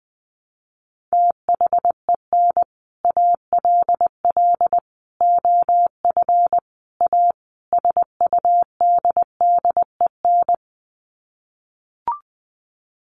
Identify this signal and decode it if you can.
Morse code, heard as a keyed tone, about 20 wpm, 700 Hz. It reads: THEN ALL OF A SUDDEN
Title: Sets of 5 Words Encore 20wpm